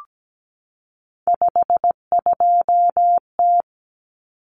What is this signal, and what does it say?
Morse code, heard as a keyed tone, about 17 wpm, 700 Hz. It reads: 52T
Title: Common Cut Numbers - 3-Digit 17wpm